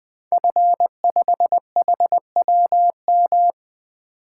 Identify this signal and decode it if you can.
Morse code, heard as a keyed tone, about 20 wpm, 700 Hz. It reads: F5HWM
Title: Code Groups - 5 Letter:Numbers - 20wpm